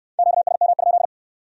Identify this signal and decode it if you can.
Morse code, heard as a keyed tone, about 62 wpm, 700 Hz. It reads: QSM?